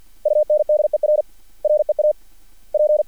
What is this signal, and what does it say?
Morse code, heard as a keyed tone, about 50 wpm, spaced as for 36 wpm, 600 Hz. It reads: UNDER REAP
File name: archived / CRSnet